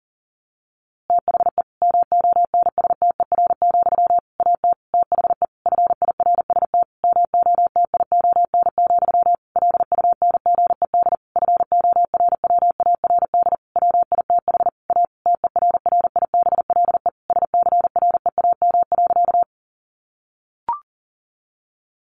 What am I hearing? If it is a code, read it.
Morse code, heard as a keyed tone, about 40 wpm, 700 Hz. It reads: THE MONSTER, AT THE FIRST MOTION, LUNGED FORWARD WITH A TERRIBLE SCREAM.
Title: A Sound Of Thunder 40wpm